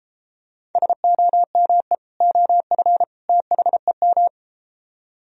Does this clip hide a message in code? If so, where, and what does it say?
Yes: Morse code from 0.8 to 4.3 s, heard as a keyed tone, about 33 wpm, 700 Hz. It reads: SOME OF THEM